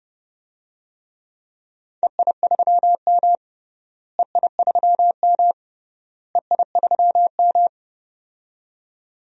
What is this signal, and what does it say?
Morse code, heard as a keyed tone, about 30 wpm, 700 Hz. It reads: EI3M EI3M EI3M
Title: DXCC-100 Call Signs - 4 Characters - Repeated 3x - 30wpm